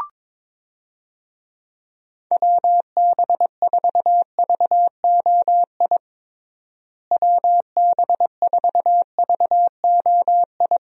Morse code, heard as a keyed tone, about 22 wpm, 700 Hz. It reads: WB4VOI WB4VOI
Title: USA Call Signs - 6 Characters - Repeated 3x - 22wpm